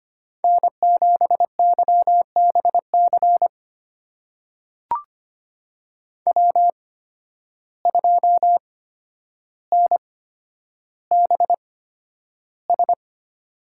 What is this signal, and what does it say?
Morse code, heard as a keyed tone, about 25 wpm, spaced as for 11 wpm, 700 Hz. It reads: N7YBC W2NBS
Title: USA Call Signs - 5 Characters - 8x Character Spacing - 25wpm